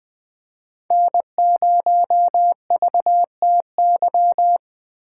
Morse code, heard as a keyed tone, about 20 wpm, 700 Hz. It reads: N0VTY
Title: Speed Racing - USA Call Signs 17wpm